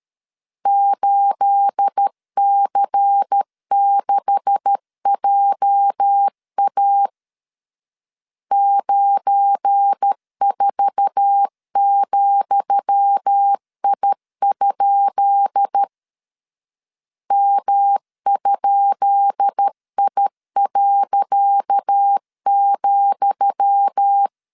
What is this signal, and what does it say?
Morse code, heard as a keyed tone, about 13 wpm, 800 Hz. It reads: 8C6JA 94,I? M?I.,
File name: LEZIONI / 10 SIGNS / 10D COMA 12wpm 2544